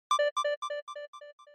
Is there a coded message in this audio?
No.